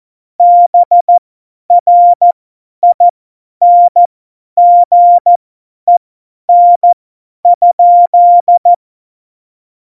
Morse code, heard as a keyed tone, about 14 wpm, spaced as for 10 wpm, 700 Hz. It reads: BRINGEN?
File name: Morsekurs / Witze / Witz 25. Februar 2020, 50 F70